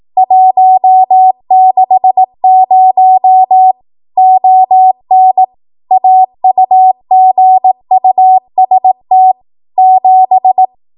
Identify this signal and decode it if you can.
Morse code, heard as a keyed tone, about 18 wpm, 750 Hz. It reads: 160 ON AUGUST 7
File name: Archive / 18 WPM / 18WPM